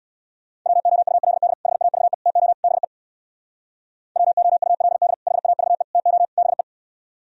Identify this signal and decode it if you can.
Morse code, heard as a keyed tone, about 60 wpm, 700 Hz. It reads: WOULD HAVE TO BE WOULD HAVE TO BE
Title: Speed-Racing - Sets of 4 Words 50wpm